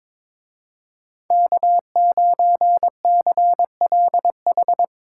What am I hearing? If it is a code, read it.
Morse code, heard as a keyed tone, about 25 wpm, 700 Hz. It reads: K9CLH
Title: Speed Racing - USA Call Signs 22wpm